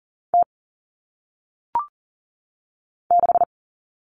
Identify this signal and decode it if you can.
Morse code, heard as a keyed tone, about 40 wpm, 700 Hz. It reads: T 6
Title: All Cut Numbers - 1-Digit 40wpm